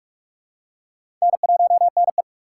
Morse code, heard as a keyed tone, about 45 wpm, 700 Hz. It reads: N1NE
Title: USA Call Signs - 4 Characters - 2x Character Spacing - 45wpm